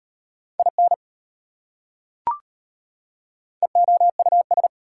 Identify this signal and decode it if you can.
Morse code, heard as a keyed tone, about 33 wpm, 700 Hz. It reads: IN EOUS